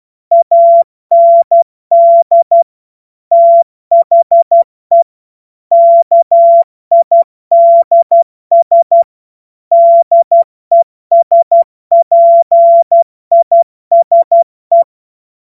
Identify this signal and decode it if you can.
Morse code, heard as a keyed tone, about 12 wpm, 680 Hz. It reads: AND THE KIDS DESPISE
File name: Morsekurs / Witze